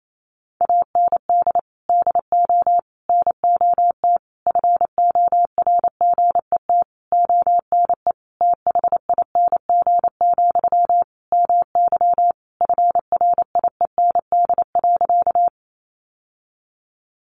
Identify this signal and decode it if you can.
Morse code, heard as a keyed tone, about 28 wpm, 700 Hz. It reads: AND DO NOT FORGET ONE THING, MY FRIEND.
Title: Sentences from Top 1200 Words 28wpm